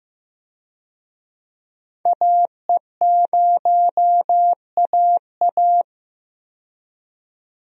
Morse code, heard as a keyed tone, about 15 wpm, 700 Hz. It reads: AE0AA
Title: USA Call Signs - 5 Characters - 15wpm